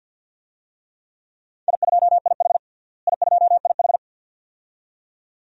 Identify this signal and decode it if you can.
Morse code, heard as a keyed tone, about 47 wpm, 700 Hz. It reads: I2IH I2IH
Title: DXCC-100 Call Signs - 4 Characters - Repeated 2x - 50wpm